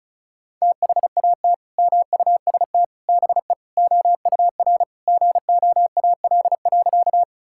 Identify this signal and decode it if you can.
Morse code, heard as a keyed tone, about 36 wpm, 700 Hz. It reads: THAT MUST BE OUR GOAL.
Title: Sentences from Top 1000 Words - Review Part 5 - 35wpm